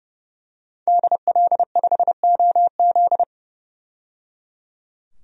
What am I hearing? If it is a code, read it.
Morse code, heard as a keyed tone, about 30 wpm, 700 Hz. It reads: DL5OZ
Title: DXCC-100 Call Signs - 5 Characters - 30wpm